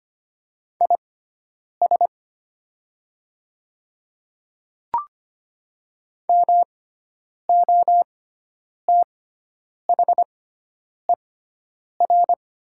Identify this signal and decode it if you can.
Morse code, heard as a keyed tone, about 25 wpm, spaced as for 9 wpm, 700 Hz. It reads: IS MOTHER
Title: 6x Character Spacing - Top 300 Words - 25wpm